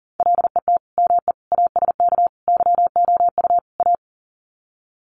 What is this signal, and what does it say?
Morse code, heard as a keyed tone, about 40 wpm, 700 Hz. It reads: LET ME ASK YOU A
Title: Sets of 5 Words Encore 40wpm